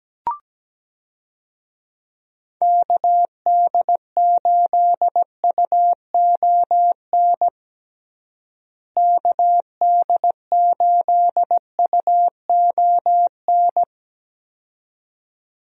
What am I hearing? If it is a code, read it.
Morse code, heard as a keyed tone, about 17 wpm, 700 Hz. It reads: KD8UON KD8UON
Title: USA Call Signs - 6 Characters - Repeated 2x - 17wpm